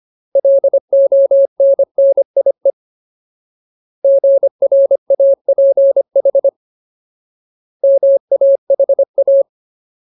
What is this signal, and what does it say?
Morse code, heard as a keyed tone, about 25 wpm, 550 Hz. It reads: LONNIE GRAPH MAHA
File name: Ultimate Word List / 25WPM Ultimate